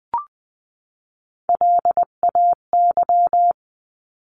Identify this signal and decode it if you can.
Morse code, heard as a keyed tone, about 20 wpm, 700 Hz. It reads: LAY
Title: 259 Lesson 42d - Words 18-20wpm